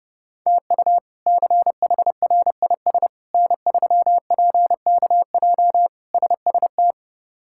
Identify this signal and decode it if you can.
Morse code, heard as a keyed tone, about 32 wpm, 700 Hz. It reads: TU CHRIS N3PKJ SST
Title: K1USN SST QSO Practice - 30wpm